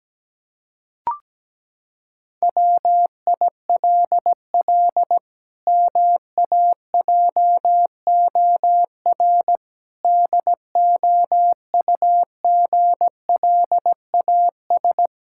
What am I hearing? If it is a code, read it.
Morse code, heard as a keyed tone, about 17 wpm, 700 Hz. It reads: WILL MAJOR DOUGLAS
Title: Pangram Challenge v2 - 17wpm